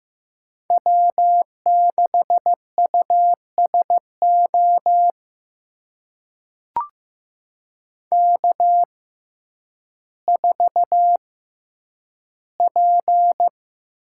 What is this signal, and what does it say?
Morse code, heard as a keyed tone, about 15 wpm, spaced as for 9 wpm, 700 Hz. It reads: W6USO K4P
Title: USA Call Signs - 5 Characters - 6x Character Spacing - 15wpm